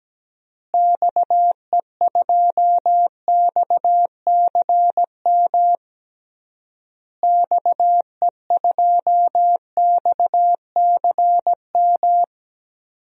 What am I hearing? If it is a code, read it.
Morse code, heard as a keyed tone, about 17 wpm, 700 Hz. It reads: XE2XCM XE2XCM
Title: DXCC-100 Call Signs - 6 Characters - Repeated 3x - 17wpm